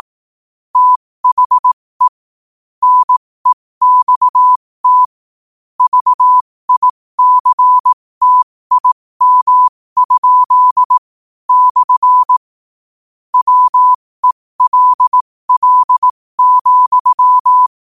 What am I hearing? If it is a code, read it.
Morse code, heard as a keyed tone, about 18 wpm, spaced as for 15 wpm, 1000 Hz. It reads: THE NEXT VICTIM? / WELL,
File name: qotdfiles / 15wpm